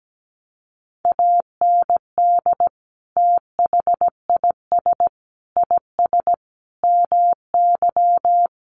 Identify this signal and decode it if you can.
Morse code, heard as a keyed tone, about 17 wpm, 700 Hz. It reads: AND THIS IS MY
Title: Sentences from Top 1200 Words 17wpm